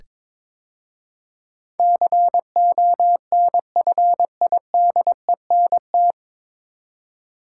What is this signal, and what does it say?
Morse code, heard as a keyed tone, about 23 wpm, 700 Hz. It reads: CONFIDENT
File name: 9 Letter Words - First 3 Letters Given - 22wpm